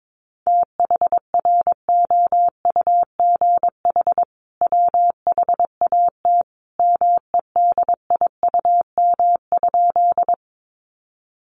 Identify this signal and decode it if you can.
Morse code, heard as a keyed tone, about 22 wpm, 700 Hz. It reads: THROUGH WHAT MEDIUM?
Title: Sentences from Top 900 Words 22wpm